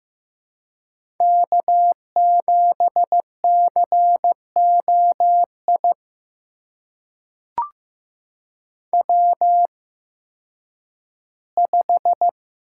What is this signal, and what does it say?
Morse code, heard as a keyed tone, about 15 wpm, spaced as for 10 wpm, 700 Hz. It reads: K7COI W5
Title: USA Call Signs - 5 Characters - 8x Character Spacing - 15wpm